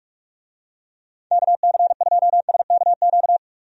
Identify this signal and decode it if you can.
Morse code, heard as a keyed tone, about 46 wpm, 700 Hz. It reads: KC2SKQ
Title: USA Call Signs - 6 Characters - 45wpm